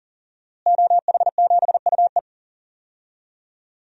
Morse code, heard as a keyed tone, about 40 wpm, 700 Hz. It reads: OH7UE
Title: DXCC-100 Call Signs - 5 Characters - Repeated 3x - 40wpm